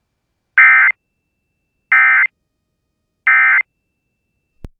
Emergency Alert System (SAME) tones, heard as harsh buzzing bursts. It is heard from 0.6 to 3.6 s.